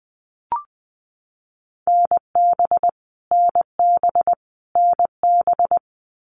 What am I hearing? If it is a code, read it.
Morse code, heard as a keyed tone, about 20 wpm, 700 Hz. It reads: NB NB NB
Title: Canadian Provinces and Territories - Repeated 3x - 20wpm